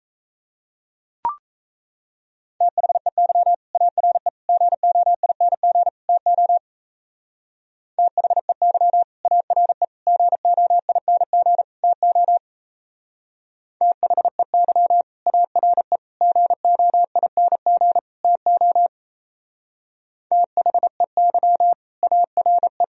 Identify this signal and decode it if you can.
Morse code, heard as a keyed tone, about 36 wpm, 700 Hz. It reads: THEY ARE GOING TO THEY ARE GOING TO THEY ARE GOING TO THEY ARE